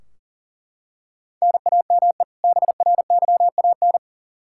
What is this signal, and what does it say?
Morse code, heard as a keyed tone, about 40 wpm, 700 Hz. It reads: NAME BRYAN
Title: K1USN SST QSO Practice 40wpm